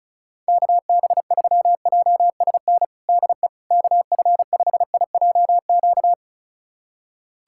Morse code, heard as a keyed tone, about 34 wpm, 700 Hz. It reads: KB3JSN DE KF5IJQ
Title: USA Callsign DE Callsign - 6 Characters - 35wpm